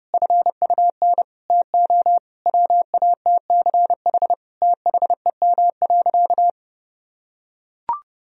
Morse code, heard as a keyed tone, about 32 wpm, 700 Hz. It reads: FUN TO WATCH THEM.
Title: Sentences from Top 1000 Words - Review Part 4 - 30wpm